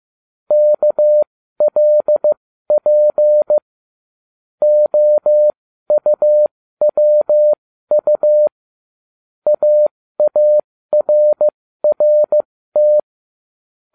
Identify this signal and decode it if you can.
Morse code, heard as a keyed tone, about 15 wpm, spaced as for 13 wpm, 600 Hz. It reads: KLP OUWU AARRT